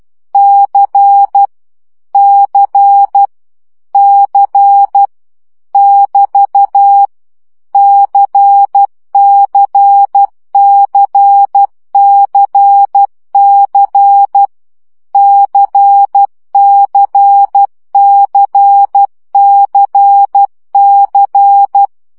Morse code, heard as a keyed tone, about 12 wpm, 800 Hz. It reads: C C C <BT> CCCCC CCCCC
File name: CW gyakorlas c 10 baud